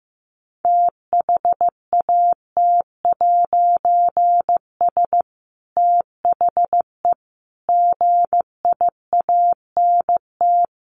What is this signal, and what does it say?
Morse code, heard as a keyed tone, about 15 wpm, 700 Hz. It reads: THAT'S THE GIANT
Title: A Sound Of Thunder 15wpm